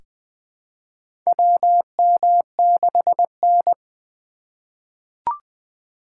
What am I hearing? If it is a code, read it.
Morse code, heard as a keyed tone, about 20 wpm, 700 Hz. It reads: WM6N
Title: USA Call Signs - 4 Characters - 4x Character Spacing - 20wpm